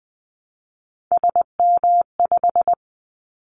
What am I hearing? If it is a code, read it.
Morse code, heard as a keyed tone, about 20 wpm, 700 Hz. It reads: SM5
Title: Callsign Prefixes for Top 25 DXCC - 20wpm